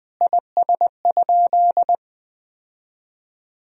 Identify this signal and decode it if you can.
Morse code, heard as a keyed tone, about 20 wpm, 700 Hz. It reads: IS?
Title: Sentences from Top 1000 Words - Review Part 1 - 20wpm